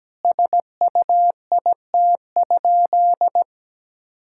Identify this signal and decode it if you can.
Morse code, heard as a keyed tone, about 17 wpm, 700 Hz. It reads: SUIT?
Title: Sentences from Top 1600 Words 17wpm